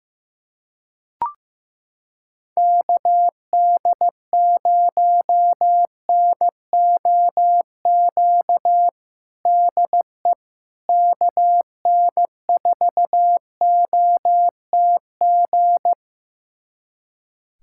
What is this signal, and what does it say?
Morse code, heard as a keyed tone, about 15 wpm, 700 Hz. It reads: KD0NOQ DE KN4OTG
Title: USA Callsign DE Callsign - 6 Characters - 15wpm